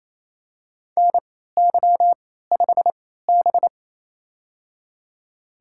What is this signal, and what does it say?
Morse code, heard as a keyed tone, about 29 wpm, spaced as for 16 wpm, 700 Hz. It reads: NY5B
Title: USA Call Signs - 3x Character Spacing - 28wpm